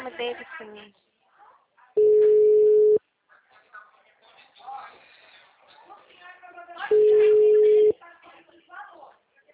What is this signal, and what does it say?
Ringback tone (Europe, 425 Hz), with speech.
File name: out- -2016- - - .3989